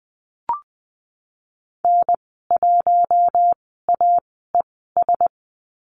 Morse code, heard as a keyed tone, about 20 wpm, 700 Hz. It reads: N 1 A E S
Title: 062 Lesson 11g - Letter h - Call Signs 20wpm